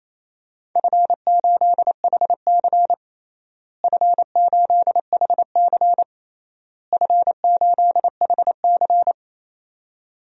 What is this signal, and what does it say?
Morse code, heard as a keyed tone, about 28 wpm, 700 Hz. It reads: F8HC F8HC F8HC
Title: DXCC-100 Call Signs - 4 Characters - Repeated 3x - 28wpm